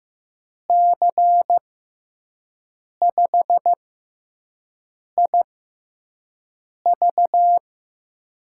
Morse code, heard as a keyed tone, about 15 wpm, spaced as for 5 wpm, 700 Hz. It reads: C5IV
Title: USA Call Signs - 5 Characters - 6x Character Spacing - 15wpm